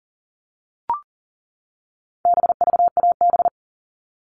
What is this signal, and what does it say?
Morse code, heard as a keyed tone, about 40 wpm, 700 Hz. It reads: BVAB
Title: All Cut Numbers - 4-Digit 40wpm